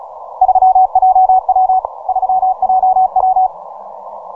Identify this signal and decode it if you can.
Morse code, heard as a keyed tone, about 36 wpm, 750 Hz. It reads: 3JW 3JW
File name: simple-file-list / VP8STI 15MCW